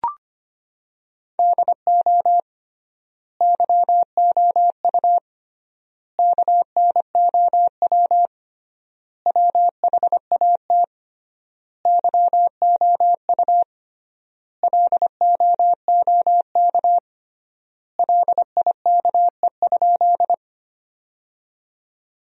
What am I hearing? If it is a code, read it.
Morse code, heard as a keyed tone, about 25 wpm, 700 Hz. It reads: DO YOU KNOW WHAT YOU LOOK LIKE?
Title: 3x Word-Spacing - Sentences from Top 100 Words - 25wpm